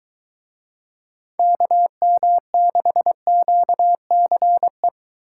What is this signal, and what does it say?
Morse code, heard as a keyed tone, about 22 wpm, 700 Hz. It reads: KM6QCE